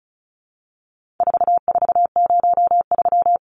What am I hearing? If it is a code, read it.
Morse code, heard as a keyed tone, about 34 wpm, 700 Hz. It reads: 4403
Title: Numbers - 4-Digit 35wpm